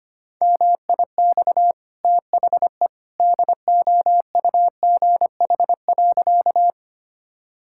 Morse code, heard as a keyed tone, about 25 wpm, 690 Hz. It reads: MIX THE DOUGH.
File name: Easy Sentences with Hints - 25wpm